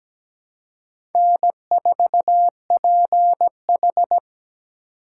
Morse code, heard as a keyed tone, about 17 wpm, 700 Hz. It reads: N4PH
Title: USA Call Signs - 4 Characters - 3x Character Spacing - 17wpm